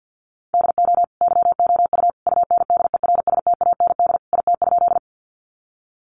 Morse code, heard as a keyed tone, about 48 wpm, 700 Hz. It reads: DO YOU UNDERSTAND IT?
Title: Sentences from Top 600 Words 50wpm